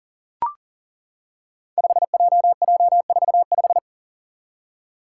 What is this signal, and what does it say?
Morse code, heard as a keyed tone, about 40 wpm, 690 Hz. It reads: 5JJ45
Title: Five Challenging Letter-Number Combinations 40wpm